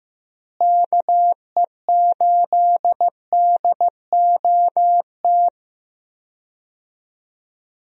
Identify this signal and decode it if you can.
Morse code, heard as a keyed tone, about 15 wpm, 700 Hz. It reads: KE8DOT
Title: USA Call Signs - 6 Characters - Repeated 3x - 15wpm